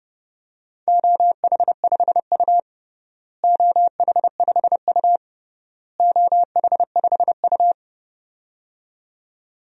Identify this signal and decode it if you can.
Morse code, heard as a keyed tone, about 30 wpm, 700 Hz. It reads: OH5U OH5U OH5U
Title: DXCC-100 Call Signs - 4 Characters - Repeated 3x - 30wpm